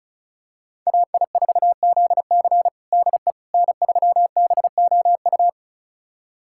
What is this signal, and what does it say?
Morse code, heard as a keyed tone, about 34 wpm, 690 Hz. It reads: AI4ZC DE N3BOU
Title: USA Callsign DE Callsign - 5 Characters - 35wpm